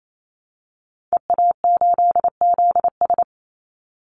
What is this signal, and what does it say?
Morse code, heard as a keyed tone, about 28 wpm, 700 Hz. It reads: EA8ZS
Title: Call Signs - 28wpm